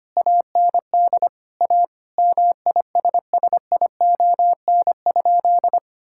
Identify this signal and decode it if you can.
Morse code, heard as a keyed tone, about 25 wpm, 700 Hz. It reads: AND A MISSION?
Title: Sentences from Top 1600 Words 25wpm